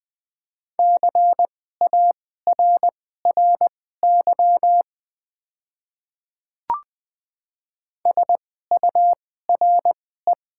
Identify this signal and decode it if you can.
Morse code, heard as a keyed tone, about 20 wpm, spaced as for 14 wpm, 700 Hz. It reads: CARRY SURE
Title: 2x Character Spacing - Top 300 Words - 20wpm